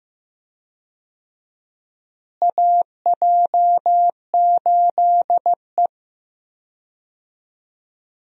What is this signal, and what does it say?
Morse code, heard as a keyed tone, about 15 wpm, 700 Hz. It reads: AJ8E
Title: USA Call Signs - 4 Characters - Repeated 2x - 15wpm